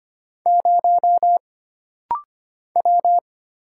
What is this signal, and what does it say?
Morse code, heard as a keyed tone, about 25 wpm, 700 Hz. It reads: 0 W
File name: Single Character (Letter:Number:Punctuation:Prosign) - Rapid-Fire - 25wpm